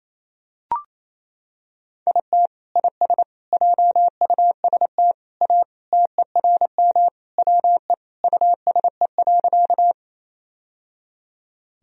Morse code, heard as a keyed tone, about 28 wpm, 700 Hz. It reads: IT IS JUST A TERM WE USE.
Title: Sentences from Top 1000 Words - Review Part 4 - 28wpm